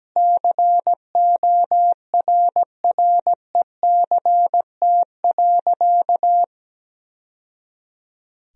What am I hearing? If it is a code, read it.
Morse code, heard as a keyed tone, about 17 wpm, 700 Hz. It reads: CORRECT.
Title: Sentences from Top 1000 Words - Review Part 2 - 17wpm